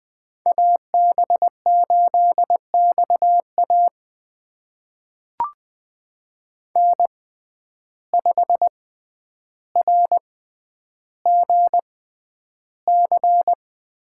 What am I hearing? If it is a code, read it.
Morse code, heard as a keyed tone, about 20 wpm, spaced as for 10 wpm, 700 Hz. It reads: AB8XA N5RGC
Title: USA Call Signs - 5 Characters - 6x Character Spacing - 20wpm